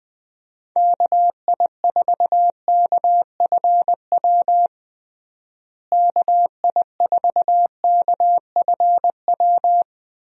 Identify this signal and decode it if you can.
Morse code, heard as a keyed tone, about 20 wpm, 700 Hz. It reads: KI4KFW KI4KFW
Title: USA Call Signs - 6 Characters - Repeated 3x - 20wpm